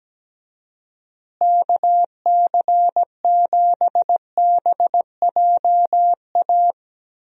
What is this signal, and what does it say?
Morse code, heard as a keyed tone, about 17 wpm, 700 Hz. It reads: KC7BJA
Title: USA Call Signs - 6 Characters - Repeated 3x - 17wpm